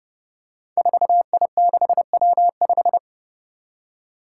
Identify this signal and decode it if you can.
Morse code, heard as a keyed tone, about 31 wpm, 700 Hz. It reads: 4I6W5
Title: Five Challenging Letter-Number Combinations 30wpm